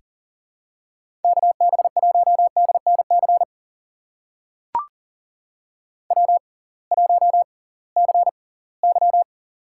Morse code, heard as a keyed tone, about 40 wpm, spaced as for 23 wpm, 700 Hz. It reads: KB1DNC W1CY
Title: USA Call Signs - 6x Character Spacing - 40wpm